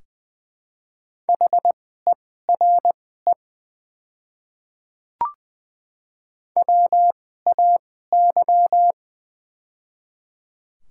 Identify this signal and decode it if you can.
Morse code, heard as a keyed tone, about 20 wpm, spaced as for 14 wpm, 700 Hz. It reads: HERE WAY
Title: 2x Character Spacing - Top 300 Words - 20wpm